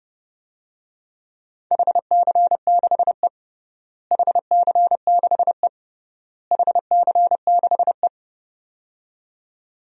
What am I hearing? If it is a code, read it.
Morse code, heard as a keyed tone, about 28 wpm, 700 Hz. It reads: HC6E HC6E HC6E
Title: DXCC-100 Call Signs - 4 Characters - Repeated 3x - 30wpm